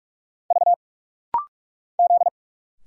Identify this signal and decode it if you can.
Morse code, heard as a keyed tone, about 46 wpm, 700 Hz. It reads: V Z
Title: Single Letter-Number Alt - Rapid-Fire 45wpm